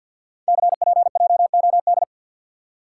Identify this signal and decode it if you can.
Morse code, heard as a keyed tone, about 50 wpm, 700 Hz. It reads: CPJOD
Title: Code Groups - 5 Letters - 50wpm